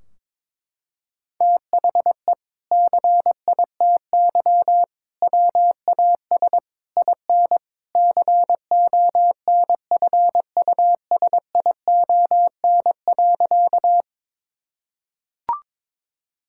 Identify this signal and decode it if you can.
Morse code, heard as a keyed tone, about 22 wpm, 700 Hz. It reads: THE CITY WAS IN CONFUSION.
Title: Sentences from Top 1600 Words 22wpm